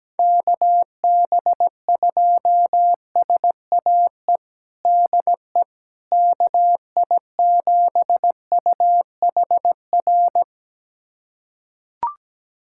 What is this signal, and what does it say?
Morse code, heard as a keyed tone, about 17 wpm, 700 Hz. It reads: KB2SAE DE KI7UHR
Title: USA Callsign DE Callsign - 6 Characters - 17wpm